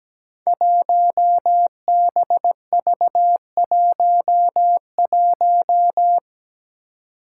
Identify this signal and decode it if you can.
Morse code, heard as a keyed tone, about 17 wpm, 700 Hz. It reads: 1BV11
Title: Code Groups - 5 Letter:Numbers - 17wpm